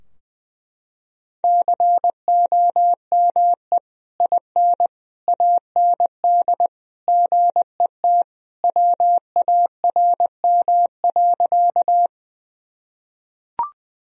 Morse code, heard as a keyed tone, about 21 wpm, 700 Hz. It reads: COME IN AND GET WARM.
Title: Sentences from Top 1000 Words - Review Part 2 - 20wpm